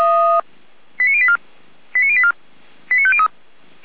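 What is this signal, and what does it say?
Five-tone selective call (ZVEI), heard as quick runs of five tones, tones of about 70 ms. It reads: 89084 89084 89503
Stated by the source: ZVEI five-tone selective call received on SDR.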